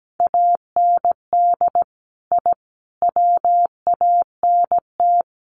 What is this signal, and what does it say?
Morse code, heard as a keyed tone, about 17 wpm, 700 Hz. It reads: AND I WANT
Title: Sets of 4 Words 17wpm